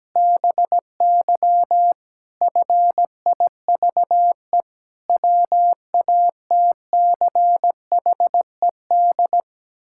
Morse code, heard as a keyed tone, about 17 wpm, 700 Hz. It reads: BY FIVE WATCHED
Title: Pangram Challenge v2 - 17wpm